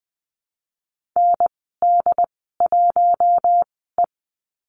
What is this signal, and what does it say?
Morse code, heard as a keyed tone, about 20 wpm, spaced as for 14 wpm, 700 Hz. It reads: ND1E
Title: 069 Lesson 12g - Letter d - Call Signs 20wpm